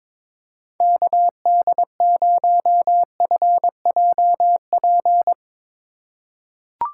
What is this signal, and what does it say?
Morse code, heard as a keyed tone, about 22 wpm, 700 Hz. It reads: KD0FJP
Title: USA Call Signs - 6 Characters - 3x Character Spacing - 22wpm